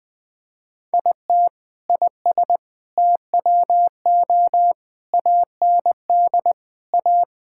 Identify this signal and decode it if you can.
Morse code, heard as a keyed tone, about 21 wpm, 700 Hz. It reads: IT IS TWO AND A
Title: Sentences with 4-Word Phrases - 20wpm